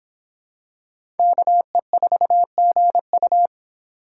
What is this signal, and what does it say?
Morse code, heard as a keyed tone, about 23 wpm, 700 Hz. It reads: KE4GU